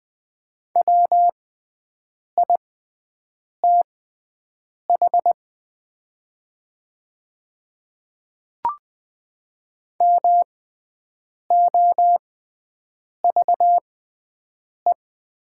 Morse code, heard as a keyed tone, about 20 wpm, spaced as for 7 wpm, 700 Hz. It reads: WITH MOVE
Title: 6x Character Spacing - Top 500 Words - 20wpm